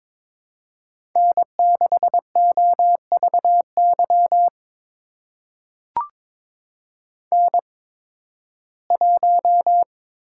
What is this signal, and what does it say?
Morse code, heard as a keyed tone, about 22 wpm, spaced as for 14 wpm, 700 Hz. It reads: N6OVY N1
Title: USA Call Signs - 5 Characters - 8x Character Spacing - 22wpm